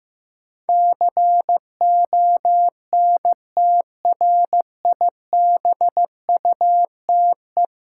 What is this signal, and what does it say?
Morse code, heard as a keyed tone, about 15 wpm, 700 Hz. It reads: CONTRIBUTE
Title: Sentences from Top 1400 Words - New Word Drill Alt - 15wpm